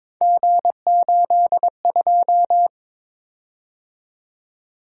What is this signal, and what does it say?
Morse code, heard as a keyed tone, about 22 wpm, 700 Hz. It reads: G82
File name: practice / 22wpm / Three Letter-Number Combinations - 22wpm